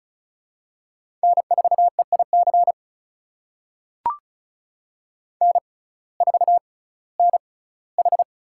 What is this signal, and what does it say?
Morse code, heard as a keyed tone, about 36 wpm, spaced as for 19 wpm, 700 Hz. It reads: N4EIC N4NH
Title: USA Call Signs - 5 Characters - 6x Character Spacing - 35wpm